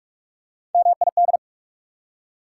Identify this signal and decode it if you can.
Morse code, heard as a keyed tone, about 47 wpm, 700 Hz. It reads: MID
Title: Prefixes and Suffixes 45wpm